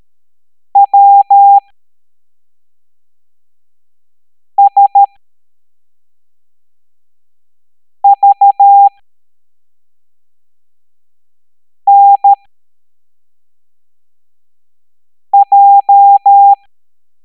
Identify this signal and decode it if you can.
Morse code, heard as a keyed tone, about 13 wpm, spaced as for 3 wpm, 800 Hz. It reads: WSVNJ